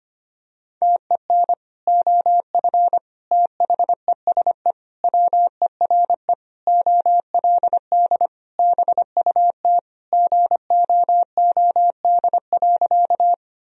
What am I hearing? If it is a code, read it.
Morse code, heard as a keyed tone, about 25 wpm, 700 Hz. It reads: TEN OF THESE WERE OLD BUT GOOD.
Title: Sentences from Top 1000 Words - Review Part 2 - 25wpm